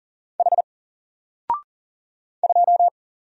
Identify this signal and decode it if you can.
Morse code, heard as a keyed tone, about 40 wpm, 700 Hz. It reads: H 2
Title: Single Letter-Number Alt 40wpm